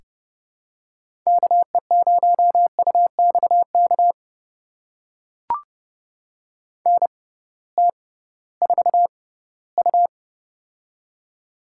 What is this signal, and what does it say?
Morse code, heard as a keyed tone, about 30 wpm, spaced as for 18 wpm, 700 Hz. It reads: KE0UXK NT4U
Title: USA Call Signs - 6x Character Spacing - 30wpm